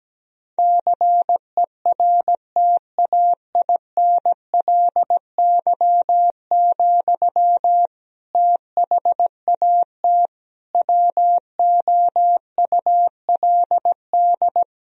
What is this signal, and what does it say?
Morse code, heard as a keyed tone, about 17 wpm, 700 Hz. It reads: CERTAINLY, THAT WOULD
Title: Sentences from Top 1000 Words - Review Part 2 - 17wpm